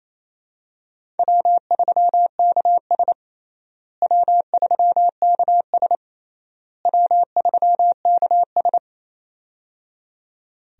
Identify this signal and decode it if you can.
Morse code, heard as a keyed tone, about 28 wpm, 700 Hz. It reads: W3KS W3KS W3KS
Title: USA Call Signs - 4 Characters - Repeated 3x - 28wpm